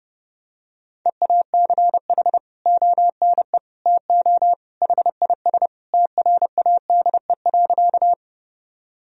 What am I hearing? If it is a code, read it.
Morse code, heard as a keyed tone, about 30 wpm, 700 Hz. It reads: EACH ONE TO HIS TRADE.
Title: Sentences from Top 1000 Words - Review Part 4 - 30wpm